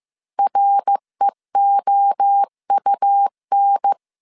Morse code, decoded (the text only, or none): REOUN